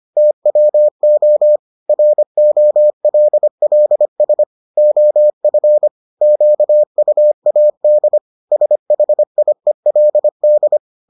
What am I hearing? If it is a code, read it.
Morse code, heard as a keyed tone, about 25 wpm, 600 Hz. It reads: TWO ROLLS OF QUAD SHIELD